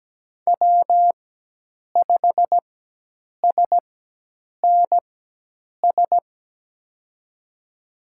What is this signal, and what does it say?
Morse code, heard as a keyed tone, about 17 wpm, spaced as for 8 wpm, 700 Hz. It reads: W5SNS
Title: USA Call Signs - 5 Characters - 4x Character Spacing - 17wpm